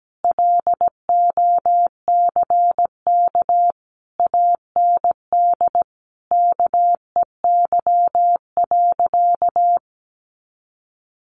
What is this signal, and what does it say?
Morse code, heard as a keyed tone, about 17 wpm, 700 Hz. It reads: LOCK AND KEY.
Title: Sentences from Top 1300 Words 17wpm